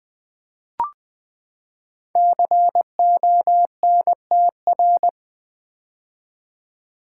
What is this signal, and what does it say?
Morse code, heard as a keyed tone, about 20 wpm, 700 Hz. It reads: CONTR
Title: Build A Word - 5 Letters At A Time - 12-Letter Words - 20wpm